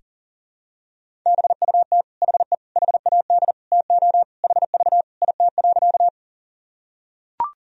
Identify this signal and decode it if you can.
Morse code, heard as a keyed tone, about 40 wpm, 700 Hz. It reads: BUT HE HAD TO HV IT.
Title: Sentences from Top 100 Words and Abbreviations 40wpm